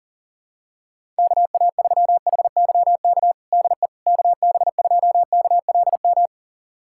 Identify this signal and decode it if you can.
Morse code, heard as a keyed tone, about 40 wpm, 700 Hz. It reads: KA3HYK DE KB2KLM
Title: USA Callsign DE Callsign - 6 Characters - 40wpm